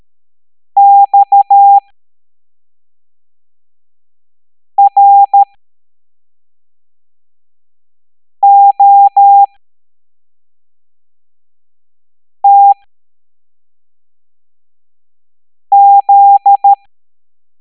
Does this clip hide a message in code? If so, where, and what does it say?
Yes: Morse code from 0.8 to 16.7 s, heard as a keyed tone, about 13 wpm, spaced as for 3 wpm, 800 Hz. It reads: XROTZ